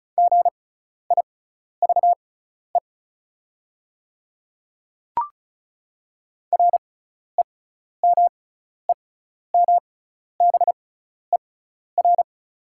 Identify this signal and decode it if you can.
Morse code, heard as a keyed tone, about 36 wpm, spaced as for 12 wpm, 690 Hz. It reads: GIVE REMEMBER
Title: 6x Character Spacing - Top 500 Words - 35wpm